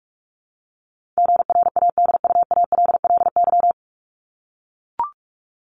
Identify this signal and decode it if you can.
Morse code, heard as a keyed tone, about 44 wpm, 700 Hz. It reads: GRADUALLY
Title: LY Suffix - Top 100 Words with Base Words 45wpm